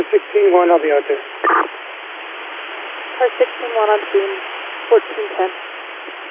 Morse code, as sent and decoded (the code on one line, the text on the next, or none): none